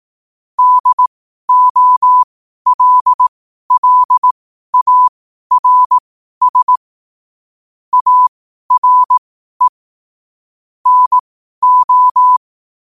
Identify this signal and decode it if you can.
Morse code, heard as a keyed tone, about 18 wpm, spaced as for 12 wpm, 1000 Hz. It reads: DOLLARS ARE NO
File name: qotdfiles / 12wpm